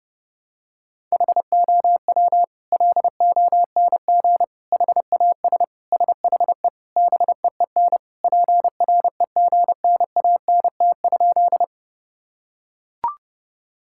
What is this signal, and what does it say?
Morse code, heard as a keyed tone, about 30 wpm, 700 Hz. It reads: HOW LONG HAS SHE BEEN PREGNANT?
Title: Sentences from Top 1000 Words - Review Part 6 - 30wpm